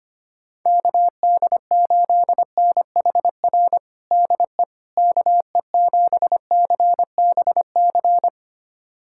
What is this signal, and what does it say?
Morse code, heard as a keyed tone, about 25 wpm, 700 Hz. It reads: KD8NHR DE KE7CBC
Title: USA Callsign DE Callsign 25wpm